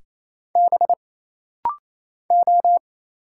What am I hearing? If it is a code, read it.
Morse code, heard as a keyed tone, about 27 wpm, 700 Hz. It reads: B O
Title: Single Letters - Rapid-Fire 28wpm